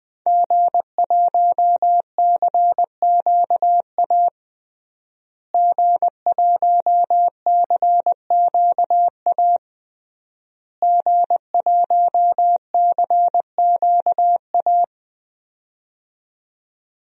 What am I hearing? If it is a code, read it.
Morse code, heard as a keyed tone, about 20 wpm, 700 Hz. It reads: G1CQA G1CQA G1CQA
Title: DXCC-100 Call Signs - 5 Characters - Repeated 3x - 20wpm